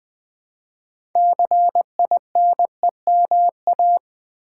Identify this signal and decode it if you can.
Morse code, heard as a keyed tone, about 20 wpm, 700 Hz. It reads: CINEMA
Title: 6 Letter Words - First 3 Letters Given - 20wpm